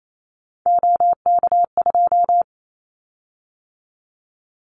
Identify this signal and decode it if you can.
Morse code, heard as a keyed tone, about 28 wpm, 700 Hz. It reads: OK2
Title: Callsign Prefixes for Top 50 DXCC - 28wpm